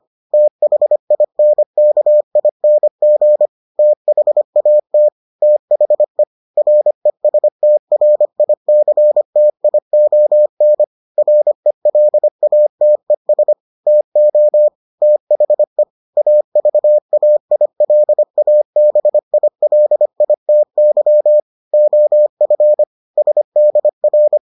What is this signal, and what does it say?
Morse code, heard as a keyed tone, about 25 wpm, 600 Hz. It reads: THINKING THAT THE RESTRICTION RELATES TO THE AVAILABILITY OF SDR